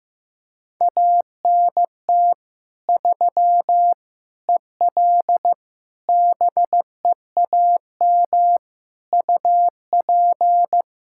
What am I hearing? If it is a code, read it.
Morse code, heard as a keyed tone, about 16 wpm, 700 Hz. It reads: ANT 3 EL BEAM UP
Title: Practice QSO - Exchange 2 - Info - 15wpm